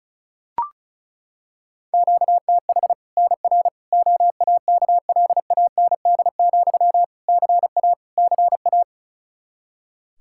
Morse code, heard as a keyed tone, about 36 wpm, 700 Hz. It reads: QTH NR OAKLAND, CA CA
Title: K1USN SST QSO Practice 35wpm